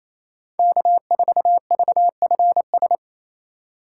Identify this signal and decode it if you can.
Morse code, heard as a keyed tone, about 28 wpm, 700 Hz. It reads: K4VFS
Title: USA Call Signs - 5 Characters - 4x Character Spacing - 28wpm